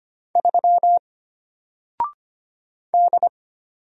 Morse code, heard as a keyed tone, about 25 wpm, 700 Hz. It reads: 3 D
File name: Single-Letter-Number-25wpm